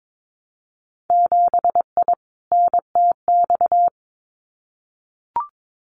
Morse code, heard as a keyed tone, about 22 wpm, 700 Hz. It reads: 7I NTX
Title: Winter Field Day Exchanges Encore 22wpm